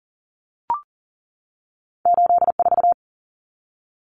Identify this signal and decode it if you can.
Morse code, heard as a keyed tone, about 40 wpm, 700 Hz. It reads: 84
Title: Common Cut Numbers - 2-Digit 40wpm